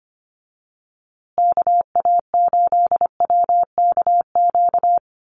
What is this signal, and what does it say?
Morse code, heard as a keyed tone, about 25 wpm, 700 Hz. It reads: KA8WKQ
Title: USA Call Signs Repeated 2x 25wpm